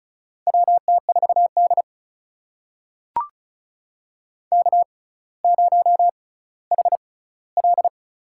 Morse code, heard as a keyed tone, about 36 wpm, spaced as for 18 wpm, 700 Hz. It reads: WT4D K0HL
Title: USA Call Signs - 4 Characters - 6x Character Spacing - 35wpm